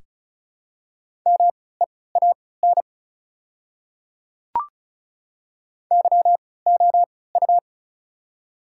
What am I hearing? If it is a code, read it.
Morse code, heard as a keyed tone, about 36 wpm, spaced as for 20 wpm, 700 Hz. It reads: MEAN YOU
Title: 3x Character Spacing - Top 200 Words - 35wpm